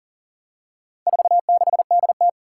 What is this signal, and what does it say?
Morse code, heard as a keyed tone, about 39 wpm, 700 Hz. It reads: 46DT
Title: Four Challenging Letter-Number Combinations 40wpm